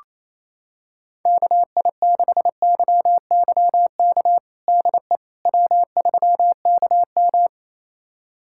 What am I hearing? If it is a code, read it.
Morse code, heard as a keyed tone, about 28 wpm, 700 Hz. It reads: KI6YYK DE W3KM
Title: USA Callsign DE Callsign 28wpm